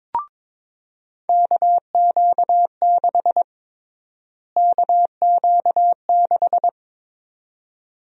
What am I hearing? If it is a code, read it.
Morse code, heard as a keyed tone, about 22 wpm, 700 Hz. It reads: KQ6 KQ6
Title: USA Call Sign Prefixes - Repeated 2x - 22wpm